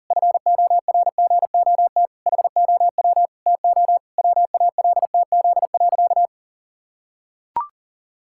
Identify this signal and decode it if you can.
Morse code, heard as a keyed tone, about 40 wpm, 700 Hz. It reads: FORGOT HOW TO WALTZ.
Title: Pangram Challenge v2 - 40wpm